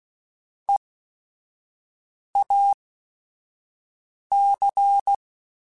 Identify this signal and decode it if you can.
Morse code, heard as a keyed tone, about 16 wpm, spaced as for 5 wpm, 800 Hz. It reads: EAC
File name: Technical Subjects / Technical Subjects-mp3s / Reactance @5